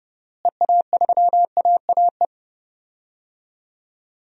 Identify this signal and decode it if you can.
Morse code, heard as a keyed tone, about 30 wpm, 700 Hz. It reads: EA3AAE
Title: DXCC-100 Call Signs - 6 Characters - Repeated 3x - 30wpm